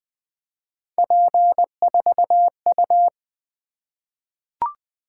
Morse code, heard as a keyed tone, about 20 wpm, 700 Hz. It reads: P4U
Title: Three Letter-Number Combinations - 20wpm